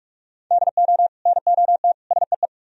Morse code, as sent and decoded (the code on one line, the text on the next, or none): -.. --- / -. --- - / ... .
DO NOT SEE